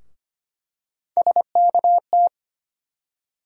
Morse code, heard as a keyed tone, about 25 wpm, 700 Hz. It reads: SKT
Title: USA Call Sign Postfixes - 3 Letters 25wpm